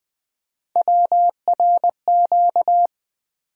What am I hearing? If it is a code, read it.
Morse code, heard as a keyed tone, about 20 wpm, 700 Hz. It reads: WRQ
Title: USA Call Sign Postfixes - 3 Letters 20wpm